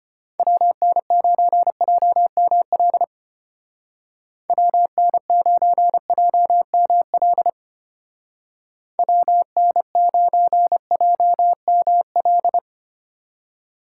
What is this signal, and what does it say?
Morse code, heard as a keyed tone, about 32 wpm, 700 Hz. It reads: WN9JML WN9JML WN9JML